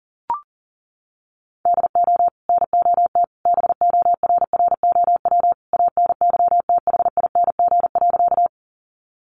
Morse code, heard as a keyed tone, about 40 wpm, 700 Hz. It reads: DO NOT BORROW ANYTHING.
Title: Sentences from Top 1500 Words 40wpm